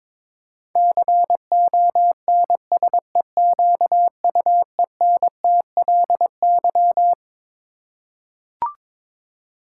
Morse code, heard as a keyed tone, about 21 wpm, 700 Hz. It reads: CONSEQUENTLY
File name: Twelve Letter Words - 22wpm